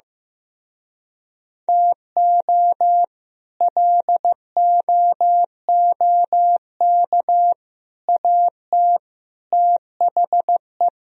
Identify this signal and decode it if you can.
Morse code, heard as a keyed tone, about 15 wpm, 700 Hz. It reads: TO LOOK AT THE
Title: Speed-Racing - Sets of 4 Words 15wpm